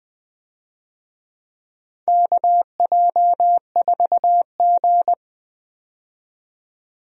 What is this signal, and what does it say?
Morse code, heard as a keyed tone, about 20 wpm, 700 Hz. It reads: KJ4G
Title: USA Call Signs - 4 Characters - 20wpm